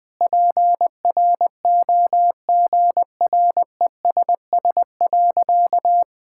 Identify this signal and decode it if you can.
Morse code, heard as a keyed tone, about 20 wpm, 700 Hz. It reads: PROGRESS.
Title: Sentences with 7-Word Phrases - Spoken Beginning - 20wpm